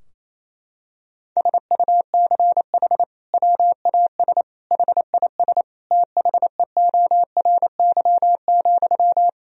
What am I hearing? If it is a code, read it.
Morse code, heard as a keyed tone, about 28 wpm, 700 Hz. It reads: SUCH WAS HIS THEORY,
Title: Sentences from Top 1000 Words - Review Part 4 - 28wpm